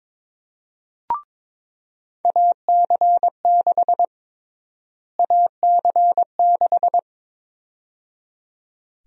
Morse code, heard as a keyed tone, about 22 wpm, 700 Hz. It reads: AC6 AC6
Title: USA Call Sign Prefixes - Repeated 2x - 22wpm